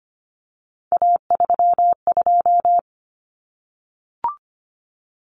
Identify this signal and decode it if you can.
Morse code, heard as a keyed tone, about 25 wpm, 700 Hz. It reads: A32
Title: Common Cut Numbers - 3-Digit 25wpm